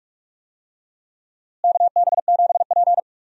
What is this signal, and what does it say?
Morse code, heard as a keyed tone, about 42 wpm, 700 Hz. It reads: KB7P
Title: USA Call Signs - 4 Characters - Repeated 3x - 45wpm